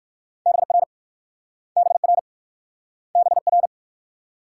Morse code, heard as a keyed tone, about 56 wpm, 700 Hz. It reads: BR BR BR
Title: Speed-Racing Multi-letter Phonemes - 45wpm